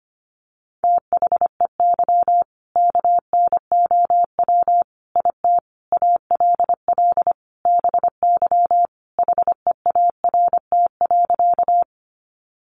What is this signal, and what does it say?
Morse code, heard as a keyed tone, about 25 wpm, 700 Hz. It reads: THEY KNOW IT ALL BY HEART.
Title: Sentences from Top 1200 Words 25wpm